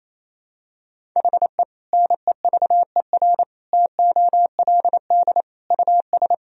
Morse code, heard as a keyed tone, about 28 wpm, 700 Hz. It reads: HE NEVER TOLD US
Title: Sentences from Top 400 Words and Abbreviations 28wpm